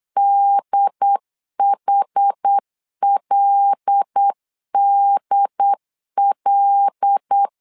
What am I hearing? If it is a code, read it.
Morse code, heard as a keyed tone, about 8 wpm, 800 Hz. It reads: DHLDL
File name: LEZIONI / 5 LBVC / 5 L 8wpm 2831